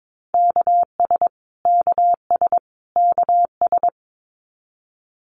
Morse code, heard as a keyed tone, about 22 wpm, 700 Hz. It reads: KS KS KS
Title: US States Repeated 3x 22wpm